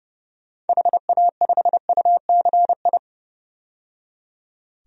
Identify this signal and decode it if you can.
Morse code, heard as a keyed tone, about 32 wpm, 700 Hz. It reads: HA5UCI
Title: DXCC-100 Call Signs - 6 Characters - 30wpm